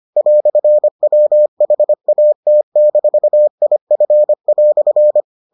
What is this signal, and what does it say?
Morse code, heard as a keyed tone, about 25 wpm, 600 Hz. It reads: "WHAT-IF"